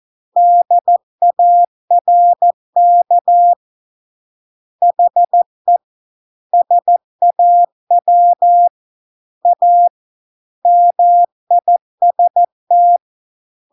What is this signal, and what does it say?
Morse code, heard as a keyed tone, about 14 wpm, 700 Hz. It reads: DARK HE SAW A MIST